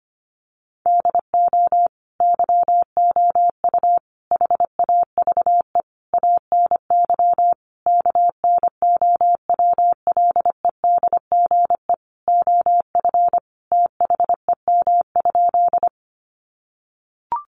Morse code, heard as a keyed tone, about 25 wpm, 700 Hz. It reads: DO YOU HAVE ANY KNOWLEDGE OF THEM?
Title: Sentences from Top 600 Words 25wpm